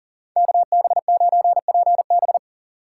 Morse code, heard as a keyed tone, about 48 wpm, 700 Hz. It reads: KB9PB
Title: Speed Racing - USA 2x2 Call Signs 40wpm